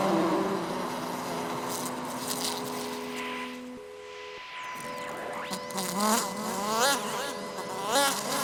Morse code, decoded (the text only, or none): none